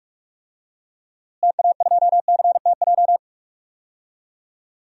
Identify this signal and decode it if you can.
Morse code, heard as a keyed tone, about 46 wpm, 700 Hz. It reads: TA2CTJ
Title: DXCC-100 Call Signs - 6 Characters - Repeated 2x - 45wpm